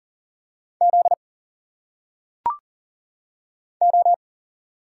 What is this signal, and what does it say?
Morse code, heard as a keyed tone, about 40 wpm, 700 Hz. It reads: Z O
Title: Single Letters - Phonetic Pronunciation 40wpm